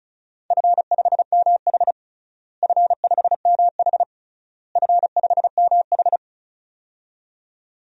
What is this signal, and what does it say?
Morse code, heard as a keyed tone, about 34 wpm, 700 Hz. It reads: F5MH F5MH F5MH
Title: DXCC-100 Call Signs - 4 Characters - Repeated 3x - 35wpm